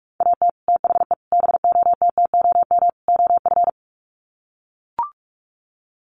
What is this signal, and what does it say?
Morse code, heard as a keyed tone, about 44 wpm, 700 Hz. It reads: AT THE BOTTOM OF
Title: Sets of 4 Words 45wpm